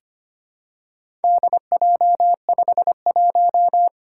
Morse code, heard as a keyed tone, about 25 wpm, 700 Hz. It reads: DJ51
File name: practice / 25wpm / Four Letter-Number Combinations - 25wpm